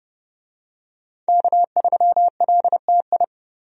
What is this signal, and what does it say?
Morse code, heard as a keyed tone, about 32 wpm, 700 Hz. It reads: K3LTI
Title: USA Call Signs - 5 Characters - 30wpm